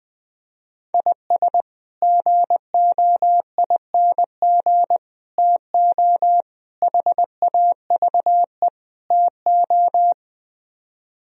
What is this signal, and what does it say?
Morse code, heard as a keyed tone, about 20 wpm, 700 Hz. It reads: IS GOING TO HAVE TO
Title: Speed-Racing - Sets of 5 Words 17wpm